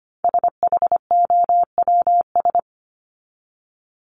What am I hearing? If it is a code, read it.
Morse code, heard as a keyed tone, about 25 wpm, 700 Hz. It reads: SHOWS
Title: ES:S Suffix - Top 100 Words with Base Words 25wpm